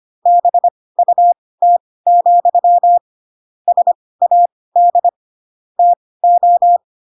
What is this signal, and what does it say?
Morse code, heard as a keyed tone, about 25 wpm, spaced as for 18 wpm, 700 Hz. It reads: BUT, SAD TO
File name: Pinocchio / Pinocchio ch4 12 25 ik1hge